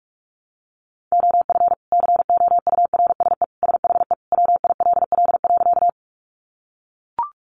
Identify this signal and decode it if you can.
Morse code, heard as a keyed tone, about 44 wpm, 700 Hz. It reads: OF COURSE SHE WILL.
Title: Sentences from Top 500 Words 45wpm